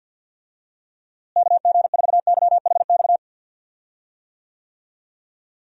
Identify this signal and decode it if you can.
Morse code, heard as a keyed tone, about 46 wpm, 700 Hz. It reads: KG4YHX
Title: USA Call Signs - 6 Characters - Repeated 3x - 50wpm